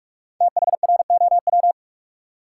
Morse code, heard as a keyed tone, about 46 wpm, 700 Hz. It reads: THROW
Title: Five Letter Words - 45wpm